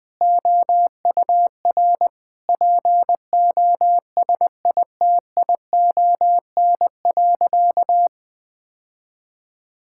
Morse code, heard as a keyed tone, about 20 wpm, 700 Hz. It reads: OUR POSITION.
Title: Sentences from Top 1000 Words - Review Part 5 - 20wpm